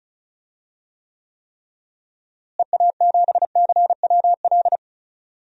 Morse code, heard as a keyed tone, about 36 wpm, 700 Hz. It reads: EA7CWL
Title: DXCC-100 Call Signs - 6 Characters - Repeated 3x - 35wpm